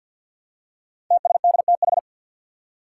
Morse code, heard as a keyed tone, about 52 wpm, 700 Hz. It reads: TSDTH
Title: Five Challenging Letter-Number Combinations 50wpm